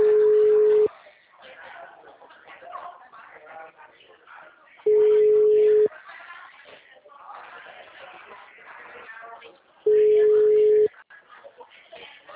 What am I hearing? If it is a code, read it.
Ringback tone (Europe, 425 Hz), with speech.